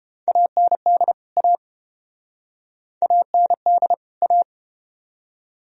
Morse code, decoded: AND A AND A

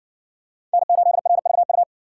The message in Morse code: N8RVU

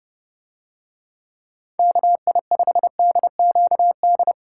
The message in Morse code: KI5DQD

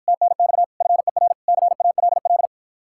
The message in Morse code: TNX FER CALL

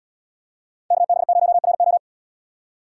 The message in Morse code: KB0RO

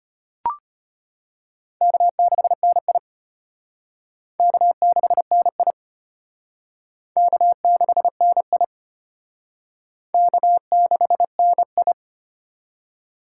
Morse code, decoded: K6NI K6NI K6NI K6NI

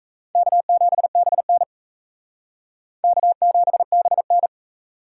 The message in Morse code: K7BN K7BN

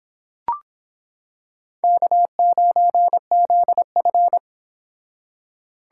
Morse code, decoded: K9ZF

